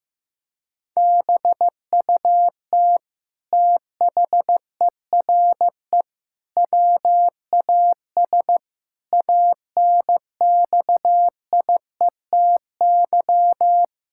Morse code, heard as a keyed tone, about 16 wpm, 700 Hz. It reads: BUT THERE WAS ANXIETY
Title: Sentences from Top 1600 Words 15wpm